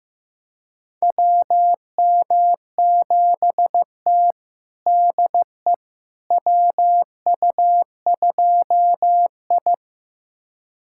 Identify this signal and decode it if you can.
Morse code, heard as a keyed tone, about 15 wpm, 700 Hz. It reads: WM7T DE WU2I
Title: USA Callsign DE Callsign - 4 Characters - 15wpm